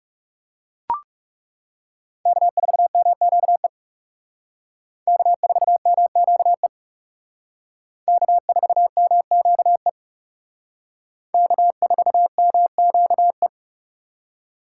Morse code, heard as a keyed tone, about 37 wpm, 700 Hz. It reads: K4MQE K4MQE K4MQE K4MQE
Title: Speed Racing - USA 1x3 Call Signs 30wpm